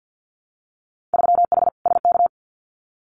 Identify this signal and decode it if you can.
Morse code, heard as a keyed tone, about 50 wpm, 700 Hz. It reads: 3H SK
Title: Winter Field Day Exchanges 50wpm